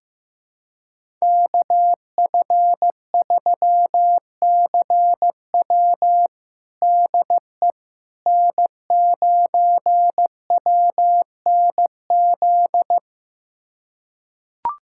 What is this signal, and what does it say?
Morse code, heard as a keyed tone, about 14 wpm, 700 Hz. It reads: KF3CW DE N9WNZ
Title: USA Callsign DE Callsign - 5 Characters - 15wpm